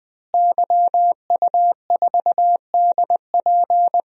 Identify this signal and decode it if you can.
Morse code, heard as a keyed tone, about 20 wpm, 700 Hz. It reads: YU4DP
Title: DXCC-100 Call Signs - 5 Characters - Repeated 2x - 20wpm